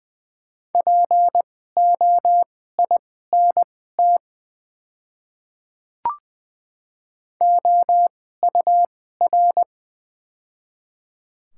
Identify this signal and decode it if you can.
Morse code, heard as a keyed tone, about 20 wpm, spaced as for 14 wpm, 700 Hz. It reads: POINT OUR
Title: 2x Character Spacing - Top 200 Words - 20wpm